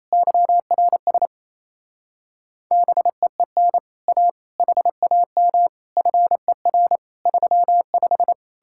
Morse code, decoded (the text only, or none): YRS BEEN A HAM FER 35